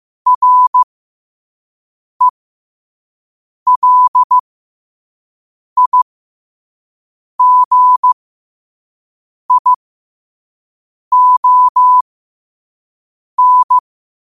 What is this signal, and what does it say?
Morse code, heard as a keyed tone, about 15 wpm, spaced as for 5 wpm, 1000 Hz. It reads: RELIGION
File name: qotdfiles / 5wpm